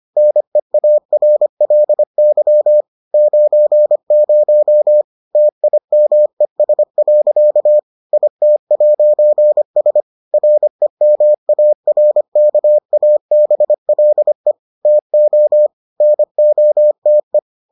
Morse code, heard as a keyed tone, about 25 wpm, 600 Hz. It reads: NEARLY 90 TIMES. IT'S REMARKABLE TO NOTE